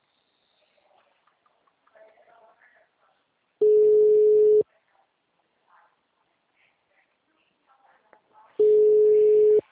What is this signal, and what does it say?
Ringback tone (Europe, 425 Hz), with speech.